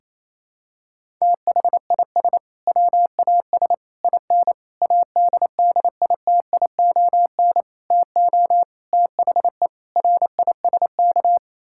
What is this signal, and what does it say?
Morse code, heard as a keyed tone, about 28 wpm, 700 Hz. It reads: THIS WAS IN ADDITION TO THE RISK
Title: Sentences from Top 1000 Words - Review Part 6 - 28wpm